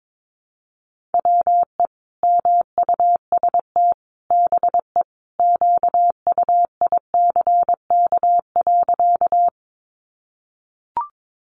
Morse code, heard as a keyed tone, about 22 wpm, 700 Hz. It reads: WE MUST BE QUICK.
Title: Sentences from Top 500 Words 22wpm